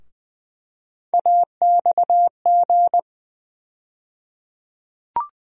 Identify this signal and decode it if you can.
Morse code, heard as a keyed tone, about 20 wpm, 700 Hz. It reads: AXG
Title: USA Call Sign Postfixes - 3 Letters 20wpm